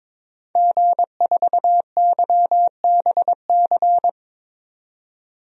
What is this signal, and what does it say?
Morse code, heard as a keyed tone, about 21 wpm, 700 Hz. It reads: G4YBC
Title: DXCC-100 Call Signs - 5 Characters - 22wpm